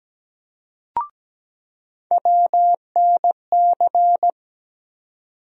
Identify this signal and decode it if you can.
Morse code, heard as a keyed tone, about 17 wpm, 700 Hz. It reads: WNC
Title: USA Call Sign Postfixes - 3 Letters 17wpm